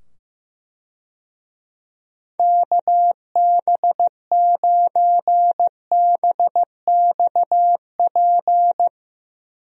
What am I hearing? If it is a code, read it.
Morse code, heard as a keyed tone, about 16 wpm, 700 Hz. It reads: KB9BXP
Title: USA Call Signs - 6 Characters - 15wpm